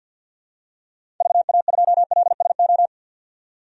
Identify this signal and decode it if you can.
Morse code, heard as a keyed tone, about 50 wpm, 700 Hz. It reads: VA2LSO
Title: DXCC-100 Call Signs - 6 Characters - 50wpm